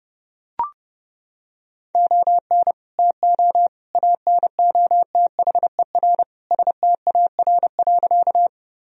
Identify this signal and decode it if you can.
Morse code, heard as a keyed tone, about 32 wpm, 700 Hz. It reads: ON TO ANOTHER STAR.
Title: Sentences from Top 1000 Words - Review Part 2 - 30wpm